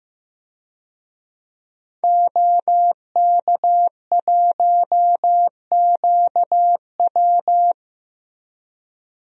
Morse code, heard as a keyed tone, about 15 wpm, 700 Hz. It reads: OK1QW
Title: DXCC-100 Call Signs - 5 Characters - Repeated 2x - 15wpm